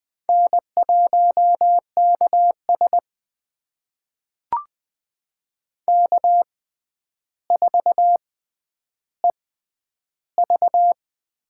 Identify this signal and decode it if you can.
Morse code, heard as a keyed tone, about 20 wpm, spaced as for 10 wpm, 700 Hz. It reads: N1KS K4EV
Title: USA Call Signs - 4 Characters - 6x Character Spacing - 20wpm